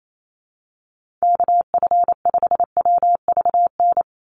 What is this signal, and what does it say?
Morse code, heard as a keyed tone, about 28 wpm, 700 Hz. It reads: KF5WVN
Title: USA Call Signs 28wpm